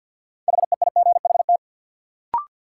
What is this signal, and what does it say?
Morse code, heard as a keyed tone, about 53 wpm, 700 Hz. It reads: HEIGHT
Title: Sets of 5 Words - New Word Drill - 50wpm